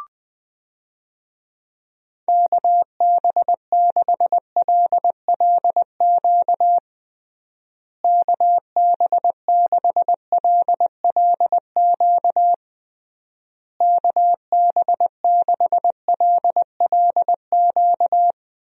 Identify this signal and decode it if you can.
Morse code, heard as a keyed tone, about 21 wpm, 700 Hz. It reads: KB6LLQ KB6LLQ KB6LLQ
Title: USA Call Signs - 6 Characters - Repeated 3x - 20wpm